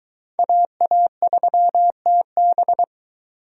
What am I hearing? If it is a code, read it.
Morse code, heard as a keyed tone, about 20 wpm, 700 Hz. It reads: AA3TB